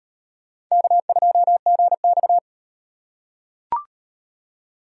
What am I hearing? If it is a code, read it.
Morse code, heard as a keyed tone, about 38 wpm, 700 Hz. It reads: K2GX